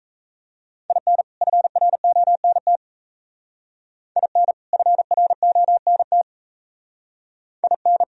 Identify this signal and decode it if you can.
Morse code, heard as a keyed tone, about 34 wpm, 700 Hz. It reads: IN FRONT IN FRONT IN
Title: Speed-Racing - Sets of 2 Words 28wpm